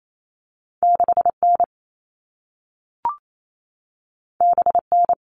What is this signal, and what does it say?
Morse code, heard as a keyed tone, about 28 wpm, 700 Hz. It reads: 6N BN
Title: All Cut Numbers - 2-Digit 28wpm